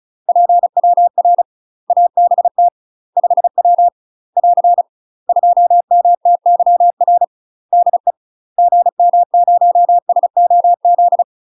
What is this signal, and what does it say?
Morse code, heard as a keyed tone, about 36 wpm, 700 Hz. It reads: PWR ABT 5W <AR> 2MTYR DE GM0SOZ
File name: UK MASTER QSOs TRAINING / 35WPM / UK MASTER QSOs 0038